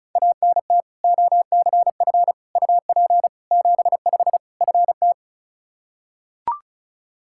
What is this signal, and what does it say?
Morse code, heard as a keyed tone, about 35 wpm, 700 Hz. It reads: ANT OCF UP 75 FT
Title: Practice QSO - Exchange 2 - Info - 35wpm